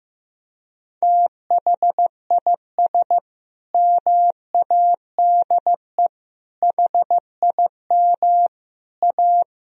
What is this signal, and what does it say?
Morse code, heard as a keyed tone, about 14 wpm, 700 Hz. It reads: THIS MADE HIM A
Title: Sentences from Top 1000 Words - Review Part 3 - 15wpm